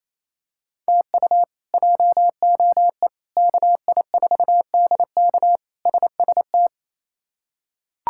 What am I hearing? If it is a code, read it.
Morse code, heard as a keyed tone, about 28 wpm, 700 Hz. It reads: TU JOE KI4DK SST
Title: K1USN SST QSO Practice - 28wpm